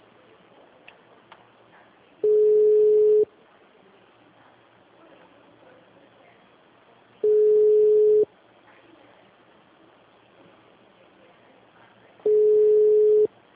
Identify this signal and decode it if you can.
Ringback tone (Europe, 425 Hz).